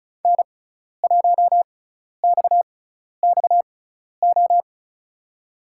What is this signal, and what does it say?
Morse code, heard as a keyed tone, about 34 wpm, spaced as for 12 wpm, 700 Hz. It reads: N1XXO
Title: USA Call Signs - 5 Characters - 6x Character Spacing - 35wpm